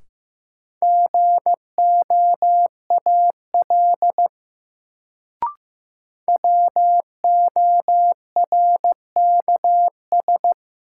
Morse code, heard as a keyed tone, about 16 wpm, 700 Hz. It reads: GOAL WORKS
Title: Sentences from Top 1000 Words - Review Part 5 - New Word Drill - 15wpm